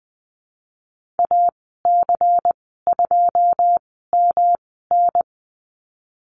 Morse code, heard as a keyed tone, about 20 wpm, spaced as for 15 wpm, 700 Hz. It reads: AC2MN
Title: 161 Lesson 26g - Letter g - Call Signs 20wpm